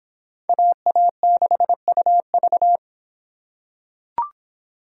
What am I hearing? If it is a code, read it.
Morse code, heard as a keyed tone, about 26 wpm, 700 Hz. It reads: AA6UV